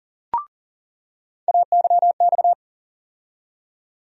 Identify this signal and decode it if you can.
Morse code, heard as a keyed tone, about 40 wpm, 700 Hz. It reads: AYX
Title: USA Call Sign Postfixes - 3 Letters 40wpm